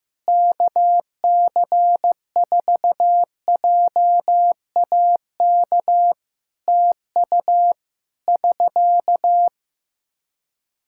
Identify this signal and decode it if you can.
Morse code, heard as a keyed tone, about 16 wpm, 700 Hz. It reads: KC4JAK TU <SK>
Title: K1USN SST QSO Practice 15wpm